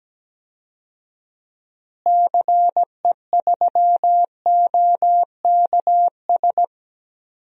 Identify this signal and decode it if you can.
Morse code, heard as a keyed tone, about 17 wpm, 700 Hz. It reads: CE3OKS
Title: DXCC-100 Call Signs - 6 Characters - Repeated 2x - 17wpm